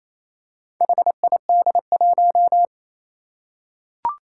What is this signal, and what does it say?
Morse code, heard as a keyed tone, about 27 wpm, 700 Hz. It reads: HID1
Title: Four Challenging Letter-Number Combinations 28wpm